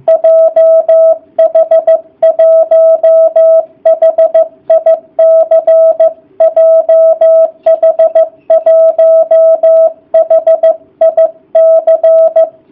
Morse code, heard as a keyed tone, about 15 wpm, 660 Hz. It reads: JH1HIC JH1HIC